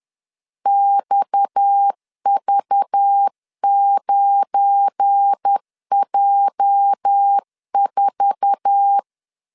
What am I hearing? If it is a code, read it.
Morse code, heard as a keyed tone, about 10 wpm, 800 Hz. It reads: XV9J4
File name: LEZIONI / 9 2867 10wpm 2733